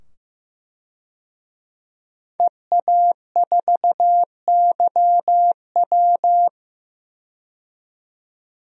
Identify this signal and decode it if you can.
Morse code, heard as a keyed tone, about 15 wpm, 700 Hz. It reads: EA4YW
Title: DXCC-100 Call Signs - 5 Characters - 15wpm